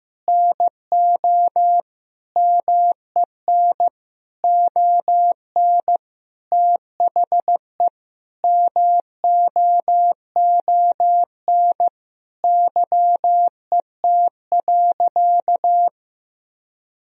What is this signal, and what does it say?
Morse code, heard as a keyed tone, about 15 wpm, 700 Hz. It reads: NO MEN ON THE MOON YET.
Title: Sentences from Top 1000 Words - Review Part 3 - 15wpm